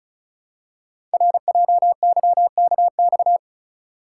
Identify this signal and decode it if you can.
Morse code, heard as a keyed tone, about 36 wpm, 700 Hz. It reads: RJYKX
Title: Code Groups - 5 Letters - 35wpm